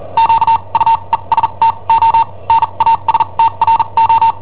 Morse code, heard as a keyed tone, about 43 wpm, spaced as for 32 wpm, 910 Hz. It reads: QUESTO NASTRO
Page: Cassetta esercizi preparata da I7OHP per IW5ELP